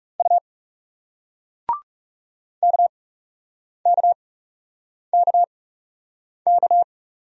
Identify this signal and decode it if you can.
Morse code, heard as a keyed tone, about 40 wpm, spaced as for 8 wpm, 700 Hz. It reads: U KKKK